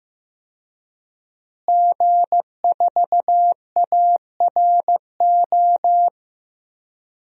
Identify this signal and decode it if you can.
Morse code, heard as a keyed tone, about 15 wpm, 700 Hz. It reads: G4ARO
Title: DXCC-100 Call Signs - 5 Characters - Repeated 2x - 15wpm